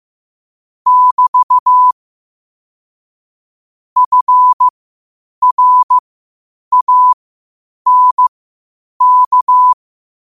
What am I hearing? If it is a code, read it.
Morse code, heard as a keyed tone, about 15 wpm, spaced as for 8 wpm, 1000 Hz. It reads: <BT> FRANK
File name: qotdfiles / 8wpm